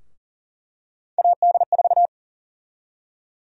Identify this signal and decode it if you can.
Morse code, heard as a keyed tone, about 40 wpm, 700 Hz. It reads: AD4
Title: USA Call Sign Prefixes - Repeated 2x - 40wpm